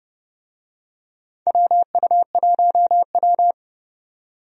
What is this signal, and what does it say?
Morse code, heard as a keyed tone, about 30 wpm, 700 Hz. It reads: WU1W
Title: USA Call Signs - 4 Characters - 30wpm